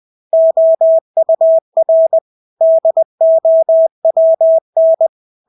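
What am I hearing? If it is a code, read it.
Morse code, heard as a keyed tone, about 20 wpm, 650 Hz. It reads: OUR DOWN
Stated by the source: Morse practice of English words at 20 wpm.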